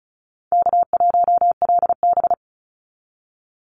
Morse code, heard as a keyed tone, about 34 wpm, 700 Hz. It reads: K1LB
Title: NA SOTA Chaser Callsigns 35wpm